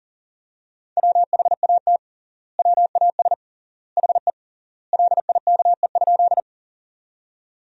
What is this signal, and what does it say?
Morse code, heard as a keyed tone, about 40 wpm, 700 Hz. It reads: WHAT WAS HE LIKE?
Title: 3x Word-Spacing - Sentences from Top 100 Words - 40wpm